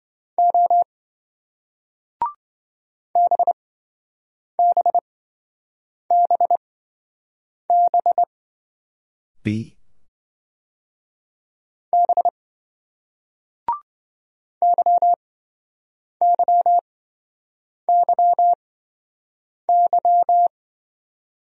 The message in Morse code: O BBBB B YYYY

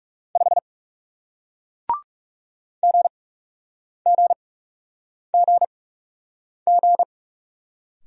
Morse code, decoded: 5 GGGG

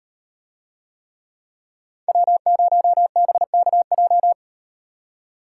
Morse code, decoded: W0BKJ

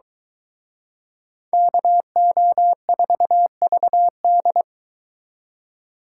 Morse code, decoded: KO4VD

